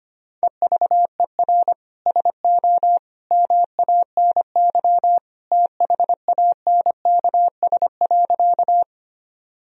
Morse code, heard as a keyed tone, about 25 wpm, 700 Hz. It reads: EVER SO MANY THANKS.
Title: Sentences from Top 1000 Words - Review Part 3 - 25wpm